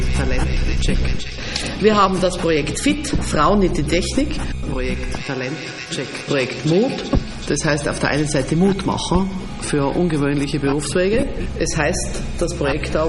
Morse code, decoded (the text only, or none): none